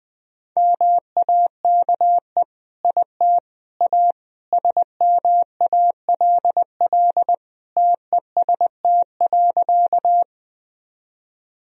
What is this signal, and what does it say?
Morse code, heard as a keyed tone, about 20 wpm, 700 Hz. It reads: MAKE IT A SMALL TEST.
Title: Sentences from Top 1000 Words - Review Part 3 - 20wpm